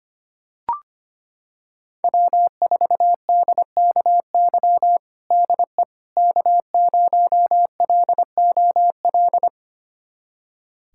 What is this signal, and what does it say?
Morse code, heard as a keyed tone, about 25 wpm, 700 Hz. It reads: W4DKY DE K0LOL
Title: USA Callsign DE Callsign - 5 Characters - 25wpm